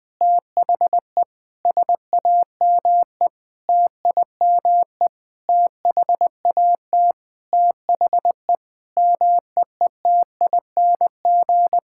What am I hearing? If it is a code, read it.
Morse code, heard as a keyed tone, about 20 wpm, 700 Hz. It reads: THE SAME TIME THAT THE MEETING
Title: Sentences with 5-Word Phrases - Spoken Beginning - 20wpm